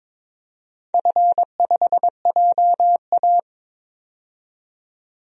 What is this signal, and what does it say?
Morse code, heard as a keyed tone, about 21 wpm, 700 Hz. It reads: F5JA
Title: DXCC-100 Call Signs - 4 Characters - 22wpm